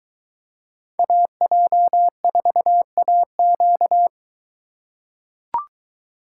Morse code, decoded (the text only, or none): AJ4AQ